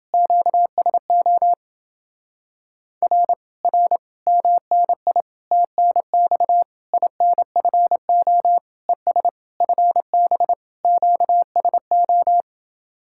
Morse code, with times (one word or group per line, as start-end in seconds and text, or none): 0.1-1.5 QSO
3.0-3.3 R
3.6-4.0 R
4.3-5.2 MNI
5.5-6.6 TNX
6.9-8.6 INFO
8.9-9.3 ES
9.6-10.5 FB
10.8-12.4 QSO